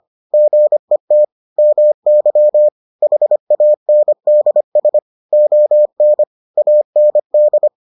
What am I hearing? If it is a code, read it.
Morse code, heard as a keyed tone, about 25 wpm, 600 Hz. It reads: GET MY HANDS ON AND